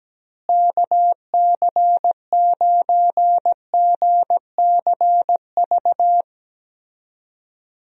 Morse code, heard as a keyed tone, about 17 wpm, 700 Hz. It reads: KC9GCV
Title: USA Call Signs - 6 Characters - Repeated 2x - 17wpm